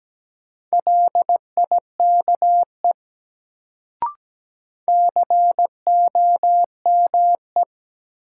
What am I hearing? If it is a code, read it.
Morse code, heard as a keyed tone, about 17 wpm, 700 Hz. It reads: LIKE COME
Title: Sentences from Top 100 Words - New Word Drill - 17wpm